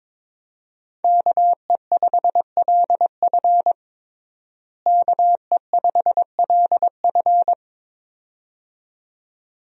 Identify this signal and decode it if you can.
Morse code, heard as a keyed tone, about 22 wpm, 700 Hz. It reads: KE5LF KE5LF
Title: USA Call Signs - 5 Characters - Repeated 2x - 22wpm